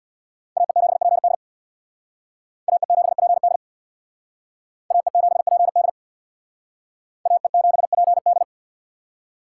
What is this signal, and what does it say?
Morse code, heard as a keyed tone, about 62 wpm, 700 Hz. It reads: AE7PD AE7PD AE7PD AE7PD
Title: Speed Racing - USA 2x2 Call Signs 50wpm